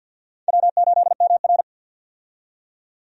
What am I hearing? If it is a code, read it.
Morse code, heard as a keyed tone, about 48 wpm, 700 Hz. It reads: W8MR
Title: USA Call Signs - 4 Characters - 50wpm